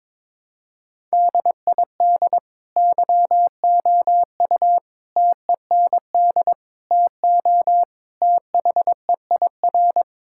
Morse code, heard as a keyed tone, about 22 wpm, 700 Hz. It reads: DID YOU TEND TO THEIR
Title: Sentences from Top 1000 Words - Review Part 5 - 22wpm